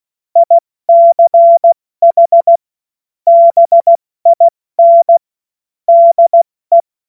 Morse code, heard as a keyed tone, about 16 wpm, 680 Hz. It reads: ICH BIN DE